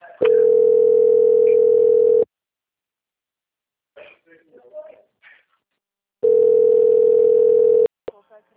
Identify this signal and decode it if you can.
Ringback tone (North America), 440 + 480 Hz, with speech.
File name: out- - - - - .32012